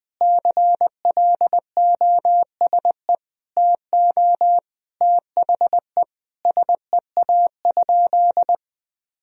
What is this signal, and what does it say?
Morse code, heard as a keyed tone, about 20 wpm, 700 Hz. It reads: CLOSE TO THE SEA?
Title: Sentences from Top 1000 Words - Review Part 1 - 20wpm